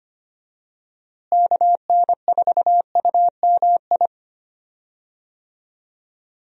Morse code, heard as a keyed tone, about 25 wpm, 700 Hz. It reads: KN4UMI
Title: USA Call Signs - 6 Characters - Repeated 3x - 25wpm